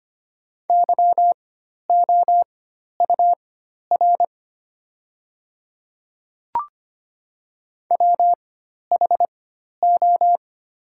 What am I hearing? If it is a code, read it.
Morse code, heard as a keyed tone, about 25 wpm, spaced as for 12 wpm, 700 Hz. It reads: YOUR WHO
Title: 4x Character Spacing - Top 100 Words - 25wpm